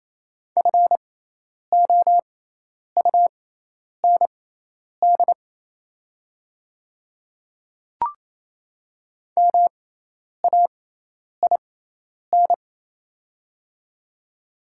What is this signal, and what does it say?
Morse code, heard as a keyed tone, about 27 wpm, spaced as for 10 wpm, 700 Hz. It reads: FOUND MAIN
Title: 6x Character Spacing - Top 300 Words - 28wpm